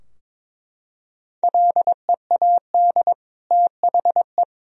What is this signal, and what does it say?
Morse code, heard as a keyed tone, about 21 wpm, 700 Hz. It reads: LEAD THE
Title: Sentences from Top 1000 Words - Review Part 2 - 22wpm